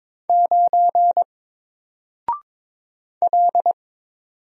Morse code, heard as a keyed tone, about 23 wpm, 700 Hz. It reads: 9 L
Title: Single Letter-Number Alt 22wpm